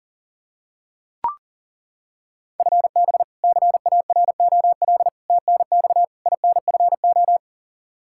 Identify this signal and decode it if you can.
Morse code, heard as a keyed tone, about 40 wpm, 700 Hz. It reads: FB CAROL TNX INFO
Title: K1USN SST QSO Practice 40wpm